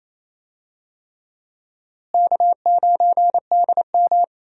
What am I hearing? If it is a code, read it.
Morse code, heard as a keyed tone, about 28 wpm, 700 Hz. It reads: K9DM
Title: USA Call Signs - 4 Characters - Repeated 3x - 28wpm